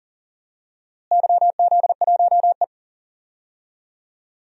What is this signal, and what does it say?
Morse code, heard as a keyed tone, about 40 wpm, 700 Hz. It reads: YZ1E
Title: DXCC-100 Call Signs - 4 Characters - Repeated 3x - 40wpm